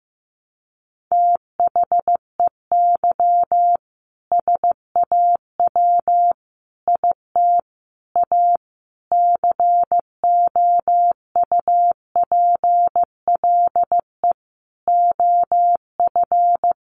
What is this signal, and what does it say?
Morse code, heard as a keyed tone, about 15 wpm, 700 Hz. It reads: THEY SAW IT A COUPLE OF
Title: Sentences from Top 1100 Words 15wpm